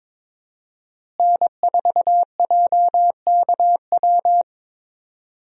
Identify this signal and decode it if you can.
Morse code, heard as a keyed tone, about 22 wpm, 700 Hz. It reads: N4JKW
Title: USA Call Signs - 5 Characters - Repeated 3x - 22wpm